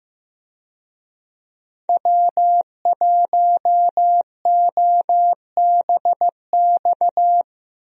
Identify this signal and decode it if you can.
Morse code, heard as a keyed tone, about 15 wpm, 700 Hz. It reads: W1OBX
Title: USA Call Signs - 5 Characters - Repeated 3x - 15wpm